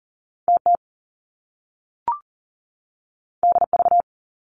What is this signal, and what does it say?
Morse code, heard as a keyed tone, about 38 wpm, 700 Hz. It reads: TT DV
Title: Top Two Letter Combinations in Words 40wpm